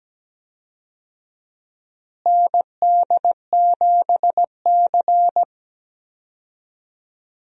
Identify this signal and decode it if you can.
Morse code, heard as a keyed tone, about 17 wpm, 700 Hz. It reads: ND7C
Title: USA Call Signs - 4 Characters - Repeated 3x - 17wpm